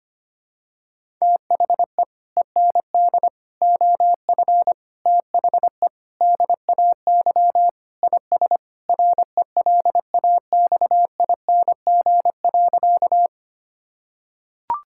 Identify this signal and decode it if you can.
Morse code, heard as a keyed tone, about 25 wpm, 700 Hz. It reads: THE END OF THE DAY IS RELAXING.
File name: practice / 25wpm / Sentences with 3-Word Phrases - 25wpm